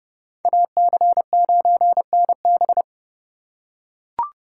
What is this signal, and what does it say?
Morse code, heard as a keyed tone, about 29 wpm, 700 Hz. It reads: AC9NB